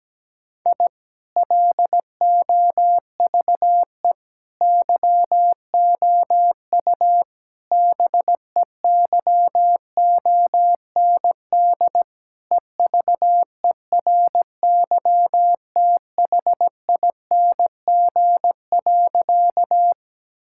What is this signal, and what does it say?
Morse code, heard as a keyed tone, about 17 wpm, 700 Hz. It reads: I LOVE YOU BEYOND EVERYTHING.
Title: Sentences from Top 1000 Words - Review Part 6 - 17wpm